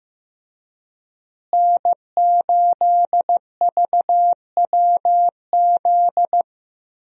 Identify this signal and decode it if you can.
Morse code, heard as a keyed tone, about 15 wpm, 700 Hz. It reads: N8VWZ
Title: USA Call Signs - 5 Characters - Repeated 3x - 15wpm